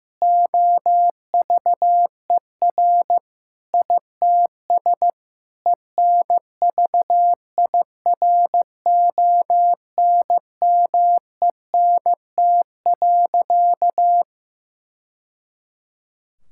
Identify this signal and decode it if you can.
Morse code, heard as a keyed tone, about 15 wpm, 700 Hz. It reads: OVER ITS ENVIRONMENT.
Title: Sentences from Top 1000 Words - Review Part 5 - 15wpm